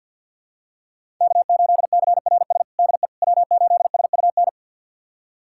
Morse code, heard as a keyed tone, about 51 wpm, 700 Hz. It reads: K8CRS DE W8SUN
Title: USA Callsign DE Callsign - 5 Characters - 50wpm